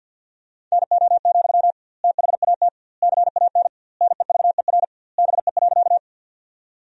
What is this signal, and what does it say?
Morse code, heard as a keyed tone, about 47 wpm, 700 Hz. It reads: NO, THAT CAN NEVER BE.
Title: 2x Word-Spacing - Sentences from Top 300 Words - 50wpm